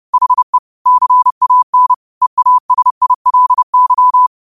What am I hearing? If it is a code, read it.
Morse code, heard as a keyed tone, about 30 wpm, 1000 Hz. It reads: HE CAN EASILY
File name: qotdfiles / 30wpm